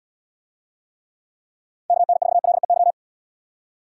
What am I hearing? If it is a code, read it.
Morse code, heard as a keyed tone, about 70 wpm, 700 Hz. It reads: KN4LEY